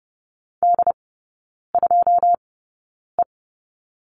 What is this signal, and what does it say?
Morse code, heard as a keyed tone, about 30 wpm, spaced as for 9 wpm, 700 Hz. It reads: D2E
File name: 081 Lesson 14e - Number 2 - Three Characters 30wpm